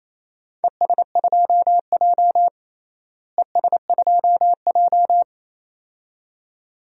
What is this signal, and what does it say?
Morse code, heard as a keyed tone, about 28 wpm, 700 Hz. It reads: ES2J ES2J
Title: DXCC-100 Call Signs - 4 Characters - Repeated 2x - 28wpm